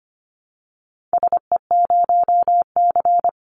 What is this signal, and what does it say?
Morse code, heard as a keyed tone, about 25 wpm, 700 Hz. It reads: SE0C
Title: Call Signs Repeated 2x - 25wpm